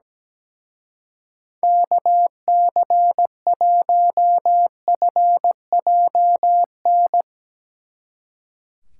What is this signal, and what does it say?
Morse code, heard as a keyed tone, about 17 wpm, 700 Hz. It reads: KC1FJN
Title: Speed Racing - USA 2x3 Call Signs 17wpm